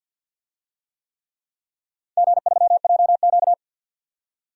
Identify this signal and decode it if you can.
Morse code, heard as a keyed tone, about 51 wpm, 700 Hz. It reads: G3JQ
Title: DXCC-100 Call Signs - 4 Characters - 50wpm